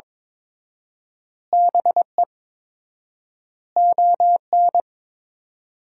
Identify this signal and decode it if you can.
Morse code, heard as a keyed tone, about 21 wpm, 700 Hz. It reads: BE ON
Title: 4x Word-Spacing - Sentences from Top 300 Words - 22wpm